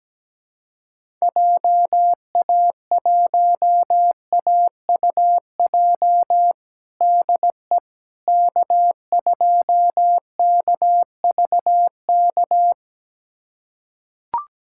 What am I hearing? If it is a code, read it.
Morse code, heard as a keyed tone, about 17 wpm, 700 Hz. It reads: JA1AUJ DE K2KVK
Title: USA Callsign DE Callsign 17wpm